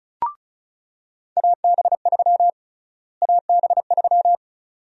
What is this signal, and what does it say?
Morse code, heard as a keyed tone, about 36 wpm, 700 Hz. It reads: AB3 AB3
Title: USA Call Sign Prefixes - Repeated 2x - 35wpm